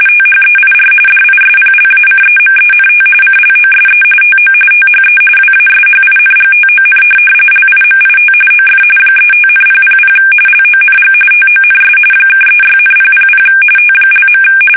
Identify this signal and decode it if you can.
RTTY, about 45.45 baud. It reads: THEIR DISPLACEMENT CLOSE TO THE WASHINGTON LIMIT OF 35,000 LONG TONS (36,000 T).
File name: fsk850 / USS-Massachusetts